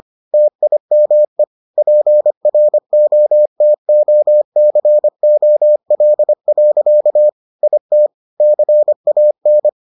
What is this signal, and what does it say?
Morse code, heard as a keyed tone, about 25 wpm, 600 Hz. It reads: TIME PROTOCOL. IT CAN